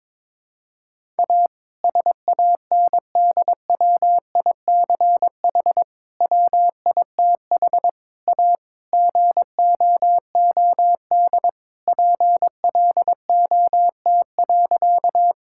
Morse code, heard as a keyed tone, about 22 wpm, 700 Hz. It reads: A SANDWICH WITH A GOOD PLOT.
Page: morse-code-ninja-practice-22wpm